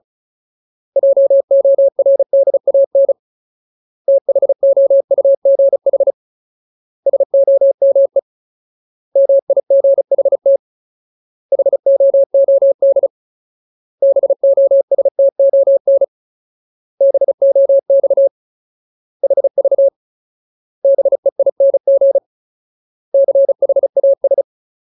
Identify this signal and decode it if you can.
Morse code, heard as a keyed tone, about 35 wpm, 550 Hz. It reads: JORDAN THOUGH SOME MIGHT HOOD BOSTON BOX HV BEING CHAS
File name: Ultimate Word List / 35WPM Ultimate